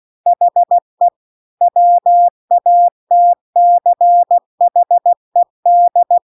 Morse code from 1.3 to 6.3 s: WATCHED